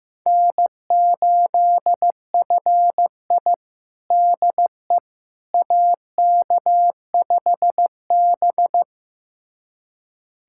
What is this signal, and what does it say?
Morse code, heard as a keyed tone, about 15 wpm, 700 Hz. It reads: N8FI DE AK5B
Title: USA Callsign DE Callsign - 4 Characters - 15wpm